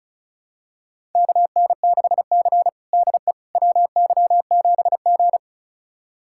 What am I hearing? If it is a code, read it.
Morse code, heard as a keyed tone, about 36 wpm, 700 Hz. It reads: KN6C DE WY7G
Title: USA Callsign DE Callsign - 4 Characters - 35wpm